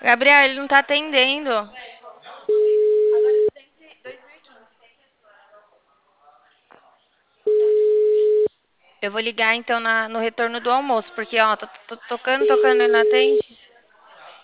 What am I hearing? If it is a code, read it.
Ringback tone (Europe, 425 Hz), with speech.
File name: out- -2014- - - .94577